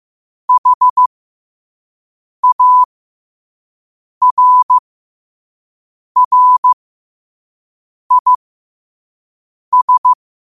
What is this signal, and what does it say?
Morse code, heard as a keyed tone, about 15 wpm, spaced as for 5 wpm, 1000 Hz. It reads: HARRIS
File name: qotdfiles / 5wpm